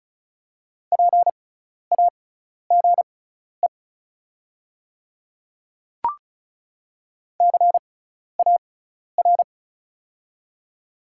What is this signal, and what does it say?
Morse code, heard as a keyed tone, about 36 wpm, spaced as for 12 wpm, 690 Hz. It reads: PAGE CAR
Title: 6x Character Spacing - Top 300 Words - 35wpm